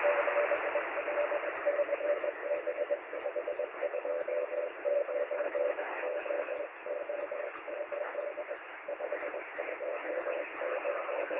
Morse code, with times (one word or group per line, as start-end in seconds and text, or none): none